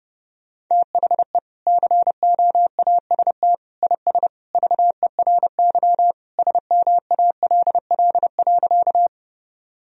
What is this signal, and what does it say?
Morse code, heard as a keyed tone, about 32 wpm, 700 Hz. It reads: THE COAST IS VERY SMALL.
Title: Sentences from Top 1000 Words - Review Part 4 - 30wpm